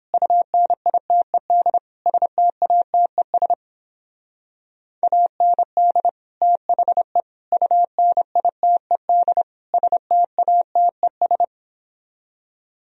Morse code, heard as a keyed tone, about 29 wpm, 700 Hz. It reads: UNITED STATES AND THE UNITED STATES